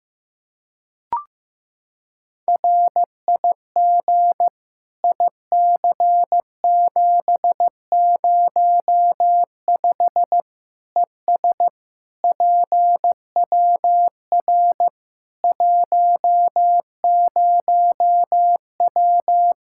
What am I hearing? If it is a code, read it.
Morse code, heard as a keyed tone, about 15 wpm, 700 Hz. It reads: RIG IC705 ES PWR 10W
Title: Practice QSO - Exchange 2 - Info - 15wpm